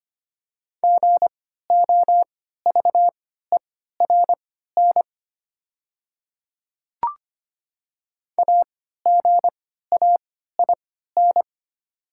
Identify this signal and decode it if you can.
Morse code, heard as a keyed tone, about 25 wpm, spaced as for 14 wpm, 700 Hz. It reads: GOVERN AGAIN
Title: 3x Character Spacing - Top 500 Words - 25wpm